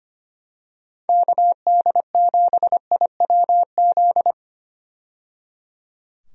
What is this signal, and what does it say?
Morse code, heard as a keyed tone, about 25 wpm, 700 Hz. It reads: KD7IWZ
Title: USA Call Signs - 6 Characters - 25wpm